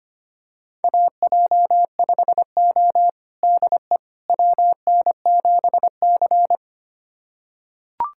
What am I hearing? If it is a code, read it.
Morse code, heard as a keyed tone, about 25 wpm, 700 Hz. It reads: AJ5O DE WN7C
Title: USA Callsign DE Callsign - 4 Characters - 25wpm